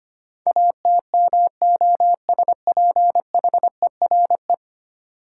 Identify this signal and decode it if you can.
Morse code, heard as a keyed tone, about 25 wpm, 700 Hz. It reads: ATMOSPHERE
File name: 10 Letter Words - First 3 Letters Given - 25wpm